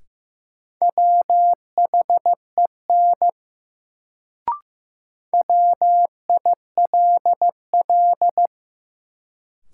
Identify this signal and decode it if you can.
Morse code, heard as a keyed tone, about 15 wpm, 700 Hz. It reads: WHEN WILL
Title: Sentences from Top 100 Words - New Word Drill - 15wpm